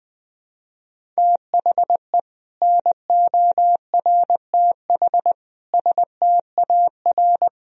Morse code, heard as a keyed tone, about 20 wpm, 700 Hz. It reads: THE NORTH STAR
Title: Sentences from Top 1000 Words - Review Part 3 - 20wpm